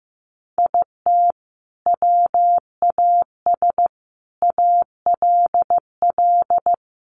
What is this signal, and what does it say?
Morse code, heard as a keyed tone, about 15 wpm, 700 Hz. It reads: IT WAS ALL
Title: Sentences from Top 600 Words 15wpm